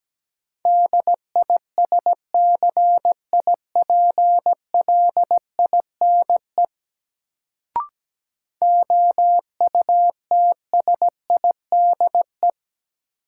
Morse code, heard as a keyed tone, about 17 wpm, 700 Hz. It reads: DISCIPLINE OUTSIDE
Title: Sentences from Top 900 Words - New Word Drill - 17wpm